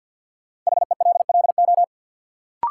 Morse code, heard as a keyed tone, about 46 wpm, 700 Hz. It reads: HELLO
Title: Sentences from Top 1100 Words - New Word Drill Alt - 50wpm